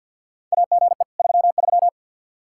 Morse code, heard as a keyed tone, about 48 wpm, 700 Hz. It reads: AGE 33
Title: Practice QSO - Exchange 3 - More Info - 50wpm